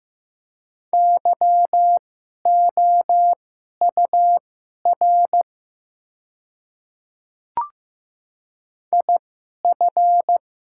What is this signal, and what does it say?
Morse code, heard as a keyed tone, about 15 wpm, spaced as for 11 wpm, 700 Hz. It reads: YOUR IF
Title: 2x Character Spacing - Top 100 Words - 15wpm